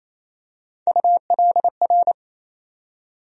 Morse code, heard as a keyed tone, about 28 wpm, 700 Hz. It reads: ULR
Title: USA Call Sign Postfixes - 3 Letters 28wpm